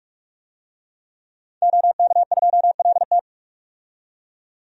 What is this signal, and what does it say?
Morse code, heard as a keyed tone, about 46 wpm, 700 Hz. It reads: OK2LT
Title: DXCC-100 Call Signs - 5 Characters - 45wpm